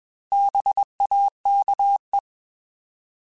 Morse code, heard as a keyed tone, about 21 wpm, 800 Hz. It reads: BAKE
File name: Bs @20